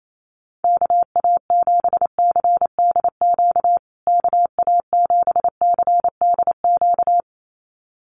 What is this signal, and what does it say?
Morse code, heard as a keyed tone, about 28 wpm, 700 Hz. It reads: KA7CDQ KA7CDQ
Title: USA Call Signs Repeated 2x 28wpm